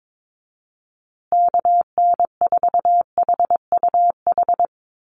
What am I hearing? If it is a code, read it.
Morse code, heard as a keyed tone, about 22 wpm, 700 Hz. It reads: KN4HUH
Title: USA Call Signs Repeated 3x 22wpm